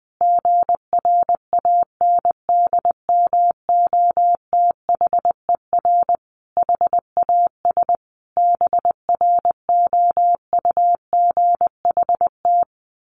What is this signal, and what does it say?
Morse code, heard as a keyed tone, about 20 wpm, 700 Hz. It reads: GRANDMOTHER HAS BROUGHT
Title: Sentences from Top 1200 Words 20wpm